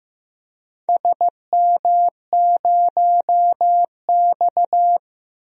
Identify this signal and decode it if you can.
Morse code, heard as a keyed tone, about 15 wpm, 700 Hz. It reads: SM0X
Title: DXCC-100 Call Signs - 4 Characters - Repeated 3x - 15wpm